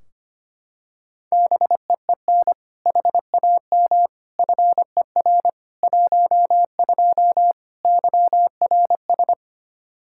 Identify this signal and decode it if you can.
Morse code, heard as a keyed tone, about 25 wpm, 700 Hz. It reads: BEEN HAM FER 12 YRS
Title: Practice QSO - Exchange 3 - More Info - 25wpm